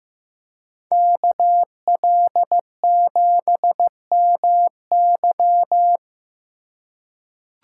Morse code, heard as a keyed tone, about 15 wpm, 700 Hz. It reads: KL7MY
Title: USA Call Signs - 5 Characters - 2x Character Spacing - 15wpm